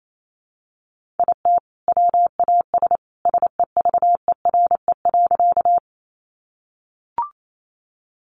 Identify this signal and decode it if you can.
Morse code, heard as a keyed tone, about 28 wpm, 700 Hz. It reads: IT WAS SEVERE.
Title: Sentences from Top 1400 Words 28wpm